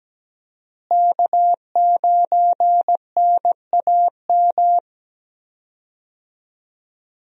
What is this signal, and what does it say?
Morse code, heard as a keyed tone, about 17 wpm, 700 Hz. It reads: K9NAM
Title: USA Call Signs - 5 Characters - Repeated 3x - 17wpm